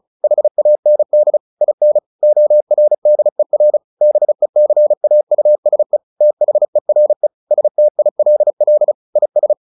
Morse code, heard as a keyed tone, about 34 wpm, 600 Hz. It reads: HAND IN ORDER BECAUSE THERE STILL IS
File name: the old man and the sea 2 35wpm